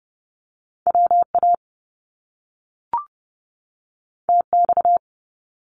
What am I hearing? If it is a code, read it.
Morse code, heard as a keyed tone, about 30 wpm, 700 Hz. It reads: WA TX
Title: US States 30wpm